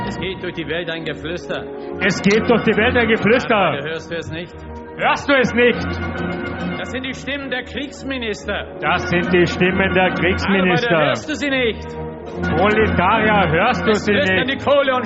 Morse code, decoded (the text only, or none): none